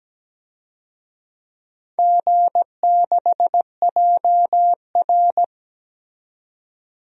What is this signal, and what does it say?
Morse code, heard as a keyed tone, about 17 wpm, 700 Hz. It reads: G6JR
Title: DXCC-100 Call Signs - 4 Characters - 17wpm